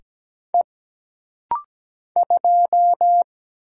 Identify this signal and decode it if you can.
Morse code, heard as a keyed tone, about 17 wpm, 700 Hz. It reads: E 2
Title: Single Letter-Number Alt - Rapid-Fire 17wpm